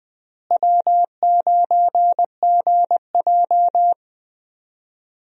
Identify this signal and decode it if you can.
Morse code, heard as a keyed tone, about 21 wpm, 700 Hz. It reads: W9GJ
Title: USA Call Signs - 4 Characters - Repeated 3x - 20wpm